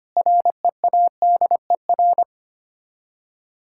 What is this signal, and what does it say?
Morse code, heard as a keyed tone, about 25 wpm, 700 Hz. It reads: READER
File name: practice / 25wpm / Difficult Words with Hints - 25wpm